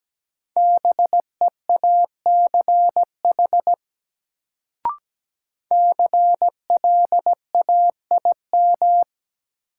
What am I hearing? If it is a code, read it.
Morse code, heard as a keyed tone, about 17 wpm, 700 Hz. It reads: BEACH CLAIM
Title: Sentences from Top 1500 Words - New Word Drill Alt - 17wpm